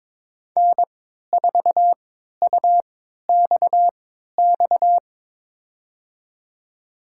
Morse code, heard as a keyed tone, about 22 wpm, spaced as for 12 wpm, 700 Hz. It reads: N4UXX
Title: USA Call Signs - 3x Character Spacing - 22wpm